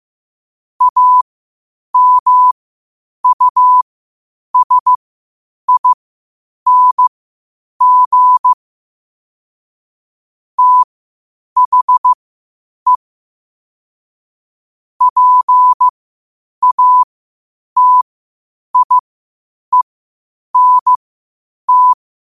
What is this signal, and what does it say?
Morse code, heard as a keyed tone, about 15 wpm, spaced as for 8 wpm, 1000 Hz. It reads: AMUSING THE PATIENT
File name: qotdfiles / 8wpm